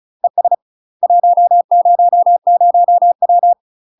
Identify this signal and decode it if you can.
Morse code, heard as a keyed tone, about 36 wpm, 700 Hz. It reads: ES 100W
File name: UK MASTER QSOs TRAINING / 35WPM / UK MASTER QSOs 0017